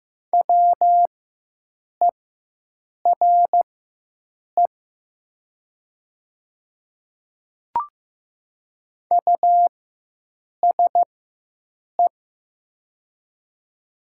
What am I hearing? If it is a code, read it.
Morse code, heard as a keyed tone, about 15 wpm, spaced as for 7 wpm, 700 Hz. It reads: WERE USE
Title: 4x Character Spacing - Top 100 Words - 15wpm